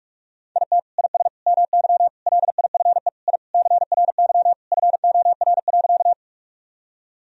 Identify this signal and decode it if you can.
Morse code, heard as a keyed tone, about 41 wpm, 700 Hz. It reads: IT IS MY LIFE I CRY FOR.
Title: Sentences from Top 1000 Words - Review Part 2 - 45wpm